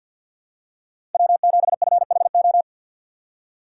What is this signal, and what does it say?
Morse code, heard as a keyed tone, about 48 wpm, 700 Hz. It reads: W7FHO
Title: USA Call Signs - 5 Characters - 6x Character Spacing - 50wpm